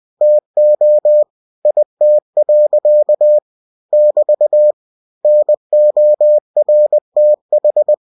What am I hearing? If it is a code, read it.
Morse code, heard as a keyed tone, about 20 wpm, 600 Hz. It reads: TO IT. <BT> NORTH